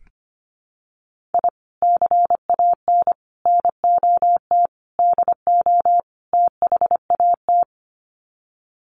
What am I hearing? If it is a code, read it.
Morse code, heard as a keyed tone, about 25 wpm, 700 Hz. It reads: I CAN NOT DO THAT
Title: Sets of 5 Words Encore2 25wpm